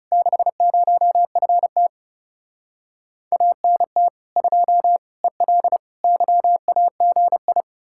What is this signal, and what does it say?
Morse code, heard as a keyed tone, about 37 wpm, 700 Hz. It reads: 60FT ANT 2 EL YAGI
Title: Speed-Racing Common QSO Phrases - 30wpm